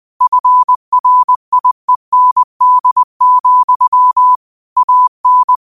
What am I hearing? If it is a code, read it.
Morse code, heard as a keyed tone, about 20 wpm, 1000 Hz. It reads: FRIEND, AN
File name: qotdfiles / 20wpm